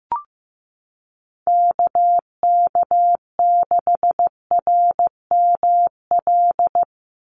Morse code, heard as a keyed tone, about 15 wpm, 700 Hz. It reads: KK6RML
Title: USA Call Signs 15wpm